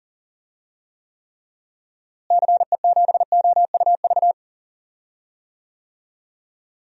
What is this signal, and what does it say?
Morse code, heard as a keyed tone, about 39 wpm, 700 Hz. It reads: CE7OUV
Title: DXCC-100 Call Signs - 6 Characters - Repeated 3x - 40wpm